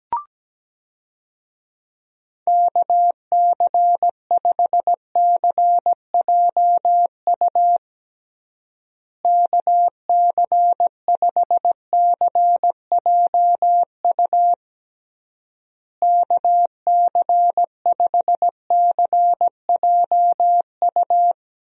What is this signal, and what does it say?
Morse code, heard as a keyed tone, about 17 wpm, 700 Hz. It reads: KC5CJU KC5CJU KC5CJU
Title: USA Call Signs - 6 Characters - Repeated 3x - 17wpm